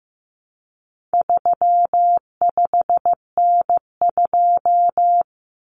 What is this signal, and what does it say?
Morse code, heard as a keyed tone, about 15 wpm, 700 Hz. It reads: 35N2
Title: Common Cut Numbers - 4-Digit 15wpm